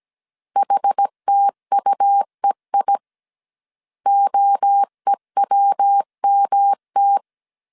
Morse code, heard as a keyed tone, about 17 wpm, 800 Hz. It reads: HTUEI OEWMT